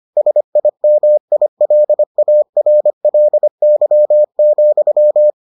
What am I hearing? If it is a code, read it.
Morse code, heard as a keyed tone, about 25 wpm, 600 Hz. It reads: SIMILARLY,